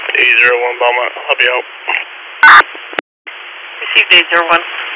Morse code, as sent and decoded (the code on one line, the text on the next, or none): none